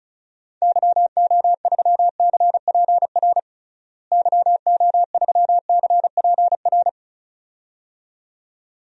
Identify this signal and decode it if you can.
Morse code, heard as a keyed tone, about 34 wpm, 700 Hz. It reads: YO3CPR YO3CPR
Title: DXCC-100 Call Signs - 6 Characters - Repeated 2x - 35wpm